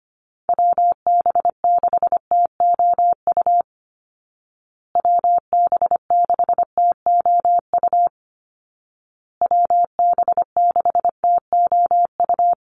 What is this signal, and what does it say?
Morse code, heard as a keyed tone, about 25 wpm, 700 Hz. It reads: WB6TOU WB6TOU WB6TOU
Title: Call Signs Repeated 2x - 25wpm